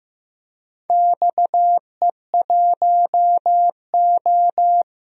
Morse code, heard as a keyed tone, about 15 wpm, 700 Hz. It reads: XE1O
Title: DXCC-100 Call Signs - 4 Characters - 15wpm